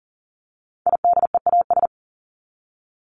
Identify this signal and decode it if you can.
Morse code, heard as a keyed tone, about 40 wpm, 700 Hz. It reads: IDEAS
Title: ES:S Suffix - Top 100 Words with Base Words 40wpm